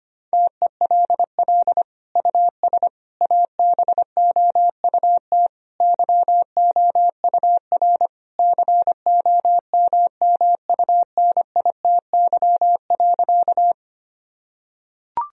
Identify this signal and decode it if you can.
Morse code, heard as a keyed tone, about 25 wpm, 700 Hz. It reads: TELL US ABOUT YOUR COMMUNITY.
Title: Sentences from Top 1000 Words - Review Part 4 - 25wpm